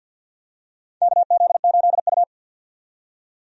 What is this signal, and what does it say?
Morse code, heard as a keyed tone, about 52 wpm, 700 Hz. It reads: KZ8U
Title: USA Call Signs - 4 Characters - 50wpm